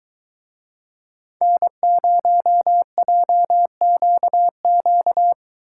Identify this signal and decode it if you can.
Morse code, heard as a keyed tone, about 22 wpm, 700 Hz. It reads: N0JQQ